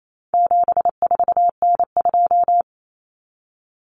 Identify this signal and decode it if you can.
Morse code, heard as a keyed tone, about 28 wpm, 700 Hz. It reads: 74N2
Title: Common Cut Numbers - 4-Digit 28wpm